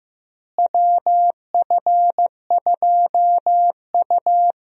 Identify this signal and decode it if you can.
Morse code, heard as a keyed tone, about 15 wpm, 700 Hz. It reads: WF2U
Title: USA Call Signs - 4 Characters - Repeated 3x - 15wpm